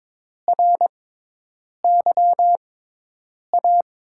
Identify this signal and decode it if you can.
Morse code, heard as a keyed tone, about 22 wpm, spaced as for 8 wpm, 700 Hz. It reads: RYA
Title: USA Call Signs - 5 Characters - 6x Character Spacing - 22wpm